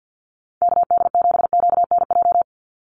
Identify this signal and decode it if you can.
Morse code, heard as a keyed tone, about 50 wpm, 700 Hz. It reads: KD7QNJ
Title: USA Call Signs 50wpm